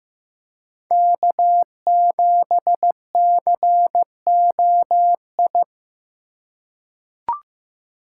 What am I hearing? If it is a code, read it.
Morse code, heard as a keyed tone, about 15 wpm, 700 Hz. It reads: K7COI
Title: USA Call Signs - 5 Characters - 4x Character Spacing - 15wpm